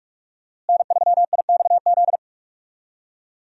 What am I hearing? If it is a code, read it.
Morse code, heard as a keyed tone, about 47 wpm, 700 Hz. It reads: N3IXZ
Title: USA Call Signs - 5 Characters - Repeated 2x - 45wpm